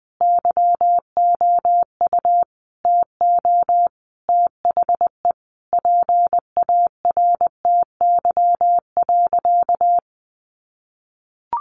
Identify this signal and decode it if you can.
Morse code, heard as a keyed tone, about 20 wpm, 700 Hz. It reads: YOU TO THE PARTY.
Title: Sentences from Top 1500 Words 20wpm